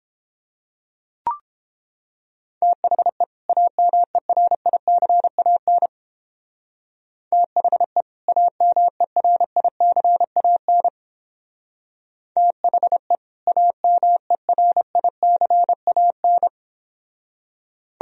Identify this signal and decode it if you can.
Morse code, heard as a keyed tone, about 29 wpm, 700 Hz. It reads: THE AMERICAN THE AMERICAN THE AMERICAN